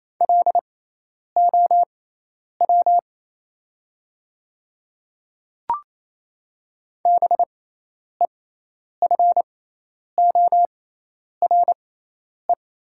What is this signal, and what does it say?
Morse code, heard as a keyed tone, about 28 wpm, spaced as for 10 wpm, 700 Hz. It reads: LOW BEFORE
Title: 6x Character Spacing - Top 200 Words - 28wpm